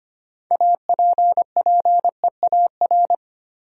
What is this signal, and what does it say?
Morse code, heard as a keyed tone, about 25 wpm, 700 Hz. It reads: APPEAR
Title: Sentences from Top 1000 Words - Review Part 2 - New Word Drill - 25wpm